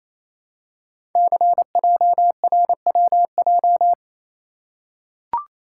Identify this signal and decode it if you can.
Morse code, heard as a keyed tone, about 27 wpm, 700 Hz. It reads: CJRWJ
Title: Code Groups - 5 Letters - 28wpm